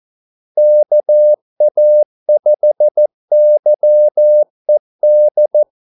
Morse code, heard as a keyed tone, about 14 wpm, 600 Hz. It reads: KA5YED
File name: Callsigns-14wpm